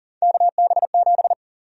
Morse code, heard as a keyed tone, about 40 wpm, 700 Hz. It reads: KB7
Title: USA Call Sign Prefixes - 40wpm